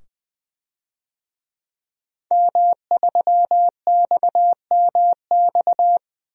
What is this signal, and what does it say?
Morse code, heard as a keyed tone, about 20 wpm, 700 Hz. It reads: M3XMX
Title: DXCC-100 Call Signs - 5 Characters - 20wpm